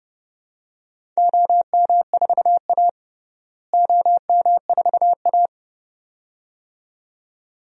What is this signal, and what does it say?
Morse code, heard as a keyed tone, about 30 wpm, 700 Hz. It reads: OM4A OM4A
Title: DXCC-100 Call Signs - 4 Characters - Repeated 2x - 30wpm